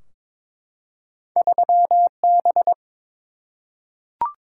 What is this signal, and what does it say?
Morse code, heard as a keyed tone, about 23 wpm, 700 Hz. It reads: 3B
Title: Single-Letter Single-Number 22wpm